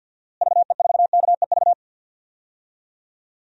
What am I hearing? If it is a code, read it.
Morse code, heard as a keyed tone, about 50 wpm, 700 Hz. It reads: VE4KEV
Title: DXCC-100 Call Signs - 6 Characters - Repeated 3x - 50wpm